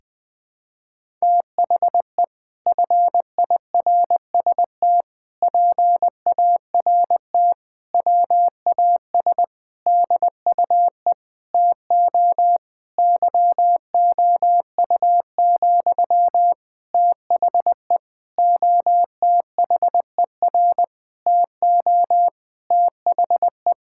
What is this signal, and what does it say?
Morse code, heard as a keyed tone, about 20 wpm, 700 Hz. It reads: THE FIRST PART WAS DUE TO YOU, THE OTHER TO THE
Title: Sentences from Top 1000 Words - Review Part 6 - 20wpm